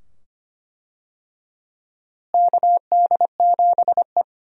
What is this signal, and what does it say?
Morse code, heard as a keyed tone, about 25 wpm, 700 Hz. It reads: KD7E
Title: USA Call Signs - 4 Characters - Repeated 2x - 25wpm